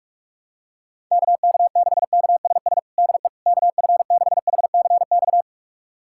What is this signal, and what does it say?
Morse code, heard as a keyed tone, about 41 wpm, 700 Hz. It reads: KK6KSS DE KF6HCX
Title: USA Callsign DE Callsign - 6 Characters - 45wpm